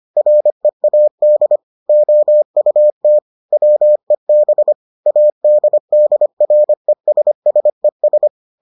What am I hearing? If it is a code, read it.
Morse code, heard as a keyed tone, about 25 wpm, 600 Hz. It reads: READ OUT WEB ADDRESSES